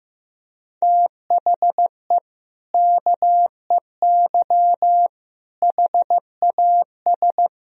Morse code, heard as a keyed tone, about 15 wpm, 700 Hz. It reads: THE KEY HAS
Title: Sentences from Top 1000 Words - Review Part 4 - 15wpm